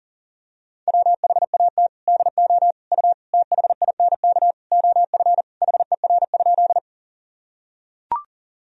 Morse code, heard as a keyed tone, about 40 wpm, 700 Hz. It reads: WHAT DO U THINK OF HER?
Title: Sentences from Top 200 Words and Abbreviations 40wpm